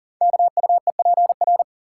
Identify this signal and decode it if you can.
Morse code, heard as a keyed tone, about 40 wpm, 690 Hz. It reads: KUEPR
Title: Code Groups - 5 Letters - 40wpm